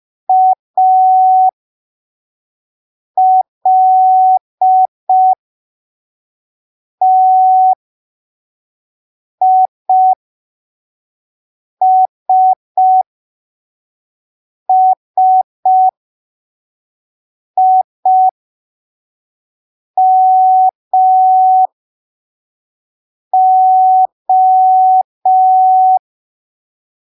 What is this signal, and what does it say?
Morse code, heard as a keyed tone, about 5 wpm, spaced as for 3 wpm, 750 Hz. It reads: ALTISSIMO
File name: arquivos / texto / texto 11 05ppm